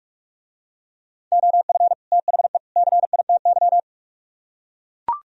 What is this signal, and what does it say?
Morse code, heard as a keyed tone, about 44 wpm, 700 Hz. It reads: OF THE CITY